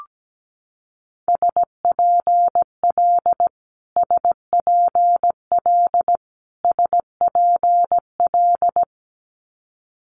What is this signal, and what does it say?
Morse code, heard as a keyed tone, about 17 wpm, 700 Hz. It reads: SPL SPL SPL
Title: Multi-letter Phonemes Repeated 3x 17wpm